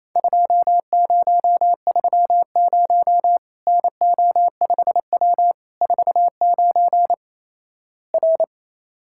Morse code, heard as a keyed tone, about 28 wpm, 700 Hz. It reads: 2030 NO5W 49 R
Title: Worked All Europe DX Contest - QTC Exchange - 28wpm